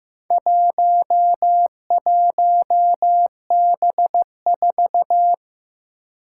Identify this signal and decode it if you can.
Morse code, heard as a keyed tone, about 15 wpm, 700 Hz. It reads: 11B4
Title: Four Challenging Letter-Number Combinations 15wpm